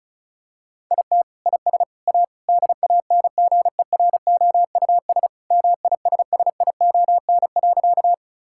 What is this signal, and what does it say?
Morse code, heard as a keyed tone, about 34 wpm, 700 Hz. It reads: IT IS A DANGEROUS MISSION.
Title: Sentences from Top 1600 Words 35wpm